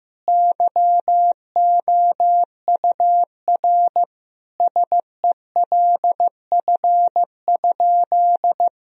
Morse code, heard as a keyed tone, about 15 wpm, 700 Hz. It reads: YOUR SELF?
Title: Sentences from Top 1000 Words - Review Part 1 - 15wpm